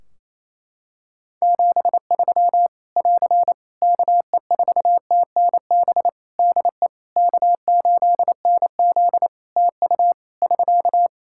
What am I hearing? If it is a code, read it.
Morse code, heard as a keyed tone, about 27 wpm, 700 Hz. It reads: 73 <AR> KE4TNB DE K8NZ TU <SK>
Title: Practice QSO - Exchange 4 - Ending - 28wpm